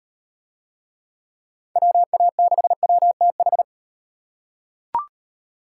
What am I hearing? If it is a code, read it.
Morse code, heard as a keyed tone, about 35 wpm, 700 Hz. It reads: WA6WTH